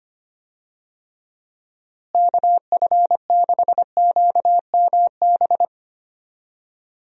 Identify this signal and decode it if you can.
Morse code, heard as a keyed tone, about 25 wpm, 700 Hz. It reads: KF6QMB
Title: USA Call Signs - 6 Characters - Repeated 3x - 25wpm